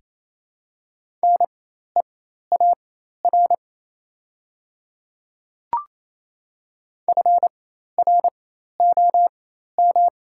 Morse code, heard as a keyed tone, about 28 wpm, spaced as for 13 wpm, 700 Hz. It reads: NEAR FROM
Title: 4x Character Spacing - Top 200 Words - 28wpm